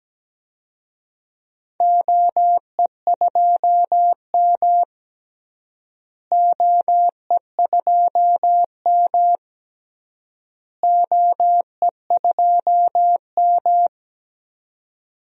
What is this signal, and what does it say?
Morse code, heard as a keyed tone, about 17 wpm, 700 Hz. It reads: OE2M OE2M OE2M
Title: DXCC-100 Call Signs - 4 Characters - Repeated 3x - 17wpm